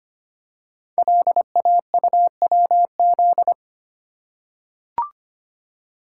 Morse code, heard as a keyed tone, about 25 wpm, 700 Hz. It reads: LAUWZ
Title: Code Groups - 5 Letter:Numbers - 25wpm